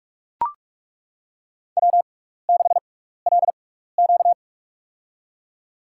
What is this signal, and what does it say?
Morse code, heard as a keyed tone, about 41 wpm, spaced as for 15 wpm, 700 Hz. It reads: W6LQ
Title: USA Call Signs - 6x Character Spacing - 45wpm